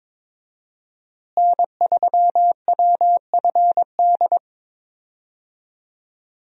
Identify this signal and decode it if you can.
Morse code, heard as a keyed tone, about 21 wpm, 700 Hz. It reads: N3WFD
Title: USA Call Signs - 5 Characters - Repeated 2x - 22wpm